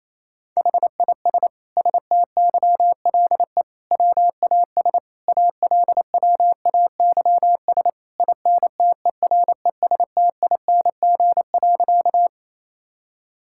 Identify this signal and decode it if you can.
Morse code, heard as a keyed tone, about 28 wpm, 700 Hz. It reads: HIS STYLE WAS ALWAYS INTERESTING.
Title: Sentences from Top 1000 Words - Review Part 5 - 28wpm